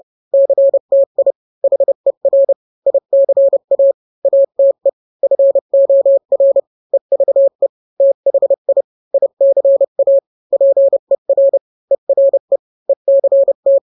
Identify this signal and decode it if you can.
Morse code, heard as a keyed tone, about 30 wpm, 550 Hz. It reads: CTI HER ICA ATE FOR EVE THI ICA PER ERE ECT